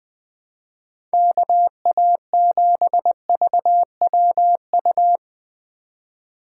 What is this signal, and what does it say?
Morse code, heard as a keyed tone, about 20 wpm, 700 Hz. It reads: KA7VWU